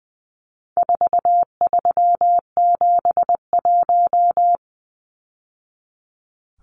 Morse code, heard as a keyed tone, about 20 wpm, 700 Hz. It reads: 4371
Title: Numbers - 4-Digit 20wpm